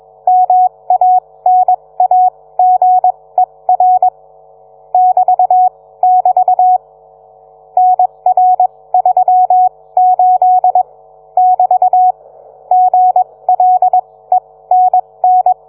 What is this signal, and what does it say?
Morse code, heard as a keyed tone, about 21 wpm, spaced as for 17 wpm, 740 Hz. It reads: MANAGER <BT><BT> NR38 <BT> GLENN